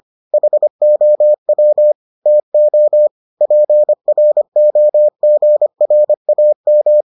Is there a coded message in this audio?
Yes: Morse code from 0.3 to 7.0 s, heard as a keyed tone, about 25 wpm, 600 Hz. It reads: HOW TO PROGRAM